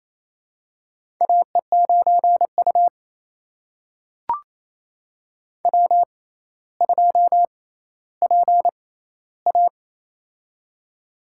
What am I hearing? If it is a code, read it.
Morse code, heard as a keyed tone, about 27 wpm, spaced as for 14 wpm, 700 Hz. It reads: AE9U W2PA
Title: USA Call Signs - 4 Characters - 6x Character Spacing - 28wpm